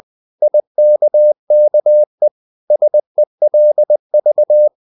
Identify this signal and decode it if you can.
Morse code, heard as a keyed tone, about 20 wpm, 600 Hz. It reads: IKKE SELV
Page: Ferie 20 wpm